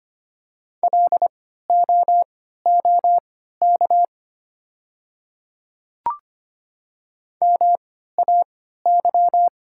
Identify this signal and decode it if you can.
Morse code, heard as a keyed tone, about 25 wpm, spaced as for 14 wpm, 700 Hz. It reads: LOOK MAY
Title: 3x Character Spacing - Top 100 Words - 25wpm